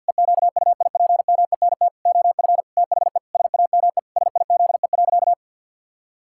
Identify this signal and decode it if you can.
Morse code, heard as a keyed tone, about 50 wpm, 700 Hz. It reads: EQUIPMENT OF THE SAME SIZE.
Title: Pangram Challenge v2 - 50wpm